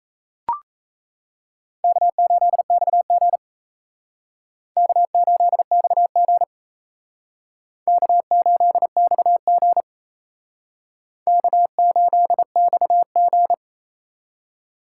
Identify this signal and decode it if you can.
Morse code, heard as a keyed tone, about 35 wpm, 700 Hz. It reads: K8XG K8XG K8XG K8XG